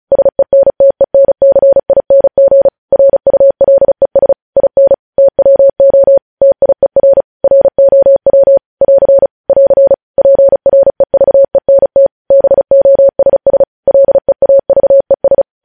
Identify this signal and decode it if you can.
Morse code, heard as a keyed tone, about 36 wpm, 550 Hz. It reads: SENTENCING RULES IN TWO TIER ROW <AR> <AR> PREVENT BOSS LEAVES
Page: News headlines in Morse code at 35 wpm